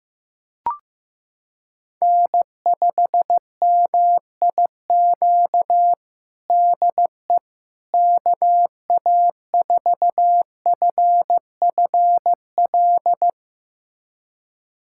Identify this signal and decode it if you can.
Morse code, heard as a keyed tone, about 15 wpm, 700 Hz. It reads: N5MIQ DE KA4FFL
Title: USA Callsign DE Callsign 15wpm